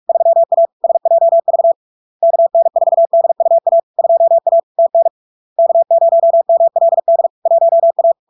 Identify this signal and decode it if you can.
Morse code, heard as a keyed tone, about 46 wpm, 670 Hz. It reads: 3A SJV KN4DUA 2A TN K0MLD 1A
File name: LARK-FD-2022-Log-in-CW-45wpm-670-Hz